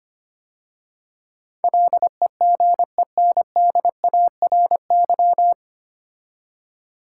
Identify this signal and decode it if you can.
Morse code, heard as a keyed tone, about 25 wpm, 700 Hz. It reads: LEGENDARY
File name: practice / 25wpm / 9 Letter Words - First 3 Letters Given - 25wpm